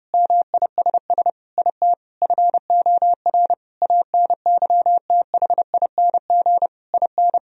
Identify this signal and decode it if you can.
Morse code, heard as a keyed tone, about 30 wpm, 700 Hz. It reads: MISS IT FOR ANYTHING IN
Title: Sentences from Top 1000 Words - Review Part 3 - 30wpm